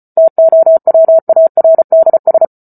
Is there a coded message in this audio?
Yes: Morse code from 0.2 to 2.5 s, heard as a keyed tone, about 34 wpm, 650 Hz. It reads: TOWARDS